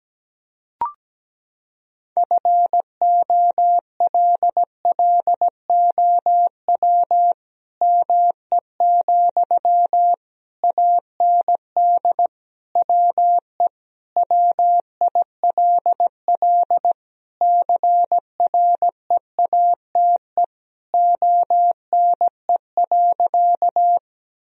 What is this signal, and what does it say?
Morse code, heard as a keyed tone, about 17 wpm, 700 Hz. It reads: FOLLOW ME, AND WE WILL CREATE ONE.
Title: Sentences from Top 1000 Words - Review Part 3 - 17wpm